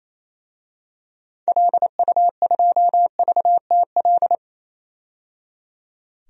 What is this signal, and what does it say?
Morse code, heard as a keyed tone, about 28 wpm, 700 Hz. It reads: LU2VTL
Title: DXCC-100 Call Signs - 6 Characters - 28wpm